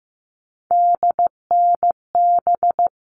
Morse code, heard as a keyed tone, about 15 wpm, 700 Hz. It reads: DNB
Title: All Cut Numbers - 3-Digit 15wpm